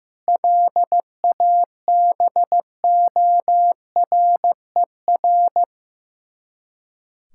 Morse code, heard as a keyed tone, about 15 wpm, 700 Hz. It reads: LABORER
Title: Practice QSO - Exchange 3 - More Info - 15wpm